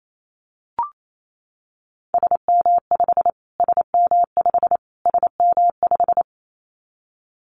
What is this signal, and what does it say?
Morse code, heard as a keyed tone, about 28 wpm, 700 Hz. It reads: SM5 SM5 SM5
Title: Callsign Prefixes for Top 25 DXCC - Repeated 3x - 28wpm